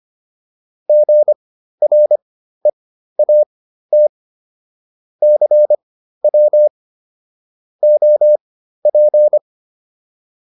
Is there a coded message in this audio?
Yes: Morse code from 0.9 to 9.4 s, heard as a keyed tone, about 25 wpm, spaced as for 13 wpm, 600 Hz. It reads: GREAT CW OP